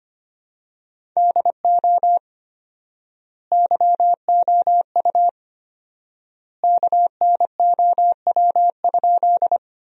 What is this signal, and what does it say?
Morse code, heard as a keyed tone, about 25 wpm, 700 Hz. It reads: DO YOU KNOW?
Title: 4x Word-Spacing - Sentences from Top 100 Words - 25wpm